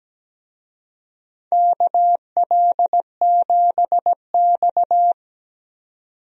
Morse code, heard as a keyed tone, about 17 wpm, 700 Hz. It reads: KL7X
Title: DXCC-100 Call Signs - 4 Characters - 17wpm